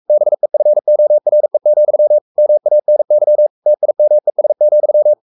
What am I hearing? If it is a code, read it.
Morse code, heard as a keyed tone, about 44 wpm, 600 Hz. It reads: BEFORE, MANY TIMES,
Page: Romanen Never af Ken Follet på engelsk.